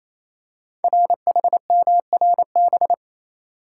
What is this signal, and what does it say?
Morse code, heard as a keyed tone, about 27 wpm, 700 Hz. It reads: RHMRB
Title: Code Groups - 5 Letters - 28wpm